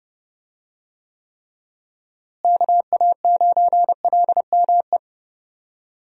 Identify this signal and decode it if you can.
Morse code, heard as a keyed tone, about 30 wpm, 700 Hz. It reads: KA9LME
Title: USA Call Signs - 6 Characters - Repeated 3x - 30wpm